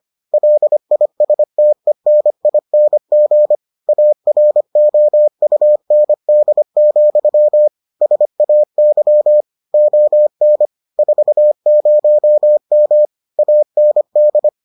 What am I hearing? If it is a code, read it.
Morse code, heard as a keyed tone, about 25 wpm, 600 Hz. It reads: LISTENING AROUND, SAY ON 40M AND